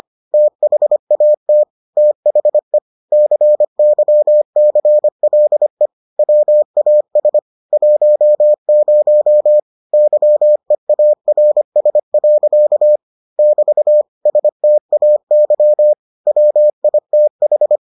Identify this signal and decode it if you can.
Morse code, heard as a keyed tone, about 25 wpm, 600 Hz. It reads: THAT THE CYCLE WAS 10 YEARS. <BT> STAY WITH